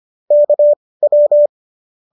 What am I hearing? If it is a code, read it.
Morse code, heard as a keyed tone, about 25 wpm, spaced as for 18 wpm, 590 Hz. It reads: KW